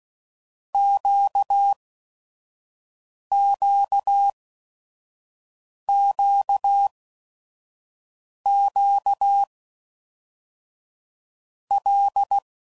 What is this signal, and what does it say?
Morse code, heard as a keyed tone, about 16 wpm, 800 Hz. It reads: Q Q Q Q L
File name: LFCYQ@5 f15